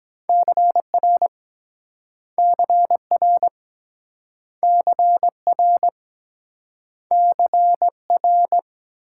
Morse code, heard as a keyed tone, about 23 wpm, 700 Hz. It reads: CR CR CR CR